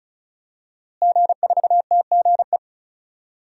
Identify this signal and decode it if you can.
Morse code, heard as a keyed tone, about 36 wpm, 700 Hz. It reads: G4TGE
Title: DXCC-100 Call Signs - 5 Characters - Repeated 3x - 35wpm